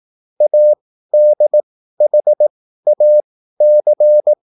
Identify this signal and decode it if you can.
Morse code, heard as a keyed tone, about 18 wpm, spaced as for 13 wpm, 600 Hz. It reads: ADHAC